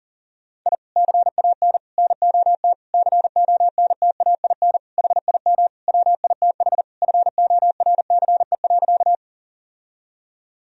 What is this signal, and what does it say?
Morse code, heard as a keyed tone, about 37 wpm, 700 Hz. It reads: I CAN NOT CONTAIN HIM WITH FORCE.
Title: Sentences from Top 1000 Words - Review Part 3 - 40wpm